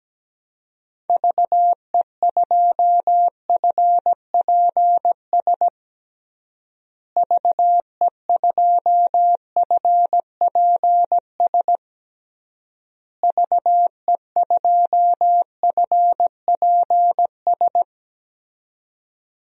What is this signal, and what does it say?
Morse code, heard as a keyed tone, about 17 wpm, 700 Hz. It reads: VE2FPS VE2FPS VE2FPS
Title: DXCC-100 Call Signs - 6 Characters - Repeated 3x - 17wpm